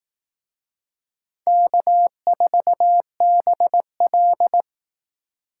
Morse code, heard as a keyed tone, about 18 wpm, 700 Hz. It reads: K4BL
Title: Speed Racing - USA 1x2 Call Signs 15wpm